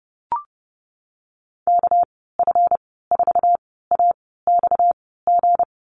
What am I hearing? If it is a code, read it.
Morse code, heard as a keyed tone, about 30 wpm, spaced as for 17 wpm, 700 Hz. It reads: KF4AXG
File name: Lesson 37g - Letter x - Call Signs 30wpm